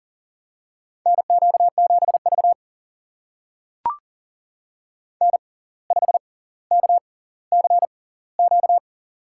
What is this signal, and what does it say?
Morse code, heard as a keyed tone, about 38 wpm, spaced as for 19 wpm, 700 Hz. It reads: NQ7V N5KCQ
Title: USA Call Signs - 6x Character Spacing - 40wpm